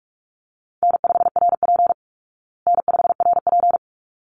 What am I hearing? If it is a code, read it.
Morse code, heard as a keyed tone, about 44 wpm, 700 Hz. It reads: N5RP N5RP
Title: Call Signs Repeated 3x - 45wpm